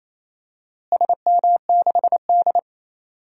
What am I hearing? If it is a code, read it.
Morse code, heard as a keyed tone, about 28 wpm, 700 Hz. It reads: SM6D
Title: DXCC-100 Call Signs - 4 Characters - Repeated 2x - 28wpm